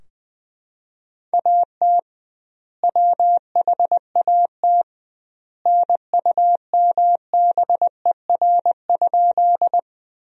Morse code, heard as a keyed tone, about 21 wpm, 700 Hz. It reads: AT WHAT NUMBER?
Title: 2x Word-Spacing - Sentences from Top 100 Words - 20wpm